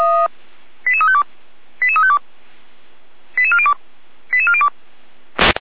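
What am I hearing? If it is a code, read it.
Five-tone selective call (ZVEI), heard as quick runs of five tones, tones of about 70 ms. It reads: 88352 88352 88402 88402
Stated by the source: ZVEI five-tone selective call received on an SDR.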